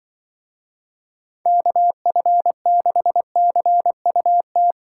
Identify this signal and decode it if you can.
Morse code, heard as a keyed tone, about 24 wpm, 700 Hz. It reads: KF6CUT
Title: Speed Racing - USA 2x3 Call Signs 20wpm